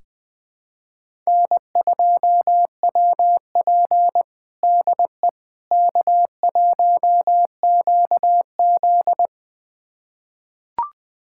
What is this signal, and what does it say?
Morse code, heard as a keyed tone, about 21 wpm, 700 Hz. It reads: N2WP DE K1QZ
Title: USA Callsign DE Callsign - 4 Characters - 20wpm